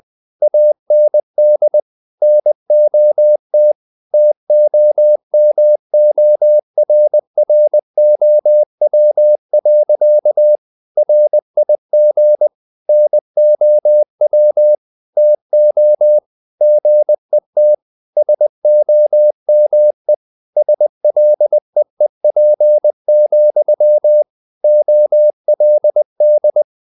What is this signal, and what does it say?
Morse code, heard as a keyed tone, about 20 wpm, 600 Hz. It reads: AND NOT TOMORROW. RIG NOW TO GET SOME SLEEP, OLD
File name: the old man and the sea part 2 20wpm